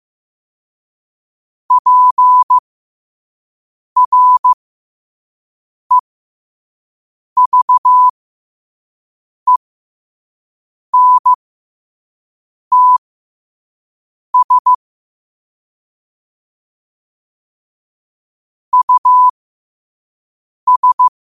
Morse code, heard as a keyed tone, about 15 wpm, spaced as for 5 wpm, 1000 Hz. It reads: PREVENTS US
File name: qotdfiles / 5wpm